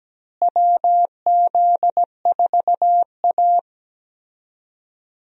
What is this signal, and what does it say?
Morse code, heard as a keyed tone, about 17 wpm, 700 Hz. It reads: WZ4A
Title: USA Call Signs - 4 Characters - Repeated 3x - 17wpm